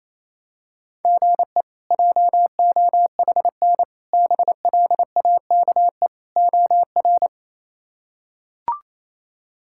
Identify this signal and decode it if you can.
Morse code, heard as a keyed tone, about 28 wpm, 700 Hz. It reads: GE JOHN BLAKE OR
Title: K1USN SST QSO Practice - 28wpm